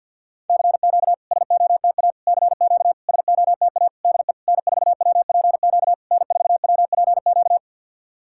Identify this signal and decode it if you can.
Morse code, heard as a keyed tone, about 50 wpm, 700 Hz. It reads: CQ SOTA CQ SOTA DE N4WPQ N4WPQ
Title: SOTA QSO Practice 50wpm